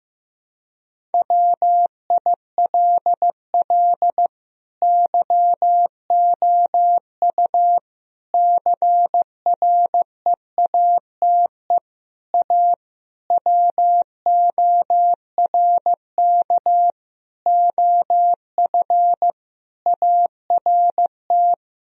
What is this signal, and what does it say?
Morse code, heard as a keyed tone, about 16 wpm, 700 Hz. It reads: WILL YOU CREATE A WORK OF ART
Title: Sentences from Top 1000 Words - Review Part 3 - 15wpm